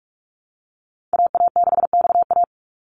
Morse code, heard as a keyed tone, about 46 wpm, 700 Hz. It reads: AA6XA
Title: NA SOTA Activator Callsigns 45wpm